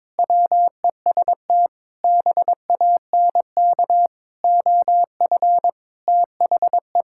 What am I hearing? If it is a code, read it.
Morse code, heard as a keyed tone, about 22 wpm, 700 Hz. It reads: WEST BANK OF THE
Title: Sentences from Top 1000 Words - Review Part 6 - 22wpm